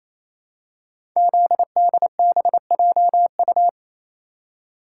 Morse code, heard as a keyed tone, about 27 wpm, 700 Hz. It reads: ZDBJU
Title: Code Groups - 5 Letters - 28wpm